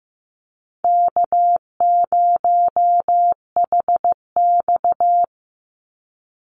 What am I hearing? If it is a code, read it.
Morse code, heard as a keyed tone, about 15 wpm, 700 Hz. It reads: K0HX
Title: Easy Call Signs Repeated 3x - 15wpm